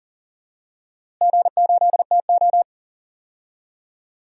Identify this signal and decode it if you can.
Morse code, heard as a keyed tone, about 40 wpm, 700 Hz. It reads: G8TO
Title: DXCC-100 Call Signs - 4 Characters - Repeated 3x - 40wpm